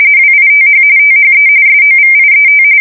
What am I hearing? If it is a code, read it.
RTTY, about 45.45 baud. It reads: WITH THE SOVIET
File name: fsk170 / usnavy5